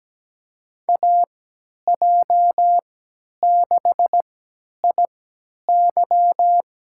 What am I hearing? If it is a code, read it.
Morse code, heard as a keyed tone, about 17 wpm, spaced as for 10 wpm, 700 Hz. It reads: AJ6IY
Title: USA Call Signs - 5 Characters - 3x Character Spacing - 17wpm